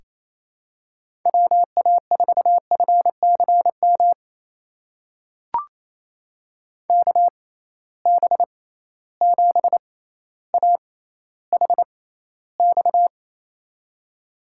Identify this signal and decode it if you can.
Morse code, heard as a keyed tone, about 28 wpm, spaced as for 14 wpm, 700 Hz. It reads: WA4FCM KB7AHX
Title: USA Call Signs - 6x Character Spacing - 28wpm